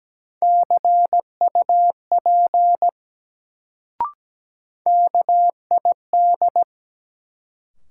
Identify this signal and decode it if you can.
Morse code, heard as a keyed tone, about 17 wpm, 700 Hz. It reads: CUP KID
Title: Sentences from Top 1600 Words - New Word Drill Alt - 17wpm